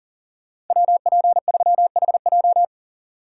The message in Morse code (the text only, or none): WP3HJ